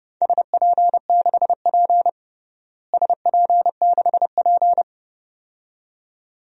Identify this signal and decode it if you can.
Morse code, heard as a keyed tone, about 30 wpm, 700 Hz. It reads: SP6P SP6P
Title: DXCC-100 Call Signs - 4 Characters - Repeated 2x - 30wpm